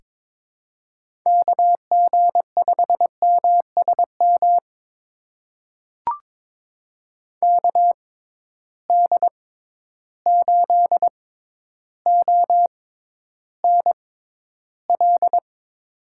Morse code, heard as a keyed tone, about 23 wpm, spaced as for 11 wpm, 700 Hz. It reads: KG5MSM KD8ONL
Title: USA Call Signs - 6 Characters - 6x Character Spacing - 22wpm